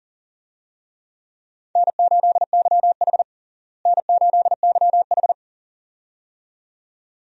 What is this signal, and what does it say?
Morse code, heard as a keyed tone, about 40 wpm, 700 Hz. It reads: N8YH N8YH
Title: USA Call Signs - 4 Characters - Repeated 2x - 40wpm